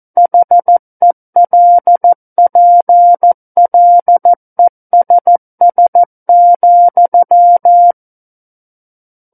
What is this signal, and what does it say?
Morse code, heard as a keyed tone, about 14 wpm, 700 Hz. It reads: HELPLESS,
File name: Book One / 05-Chapter 5 - 14 WPM